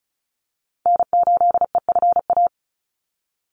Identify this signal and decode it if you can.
Morse code, heard as a keyed tone, about 34 wpm, 700 Hz. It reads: N8EFA
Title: USA Call Signs 35wpm